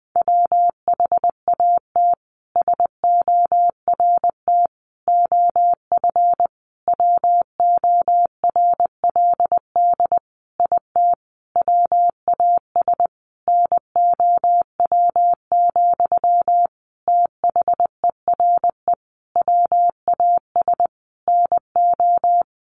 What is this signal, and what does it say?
Morse code, heard as a keyed tone, about 20 wpm, 700 Hz. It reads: WHAT SORT OF WORLD IT WAS NOW, THERE WAS NO
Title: A Sound Of Thunder 20wpm